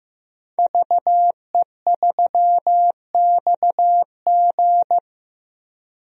Morse code, heard as a keyed tone, about 15 wpm, 700 Hz. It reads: VE3XG
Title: DXCC-100 Call Signs - 5 Characters - 15wpm